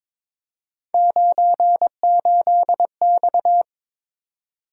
Morse code, heard as a keyed tone, about 22 wpm, 700 Hz. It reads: 98X
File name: practice / 22wpm / Three Letter-Number Combinations - 22wpm